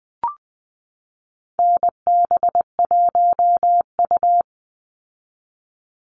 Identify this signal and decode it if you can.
Morse code, heard as a keyed tone, about 20 wpm, 700 Hz. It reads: NB1U
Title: Call Signs Repeated 3x - 20wpm